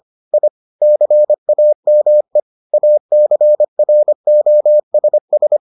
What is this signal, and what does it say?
Morse code, heard as a keyed tone, about 25 wpm, 600 Hz. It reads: I CAME ACROSS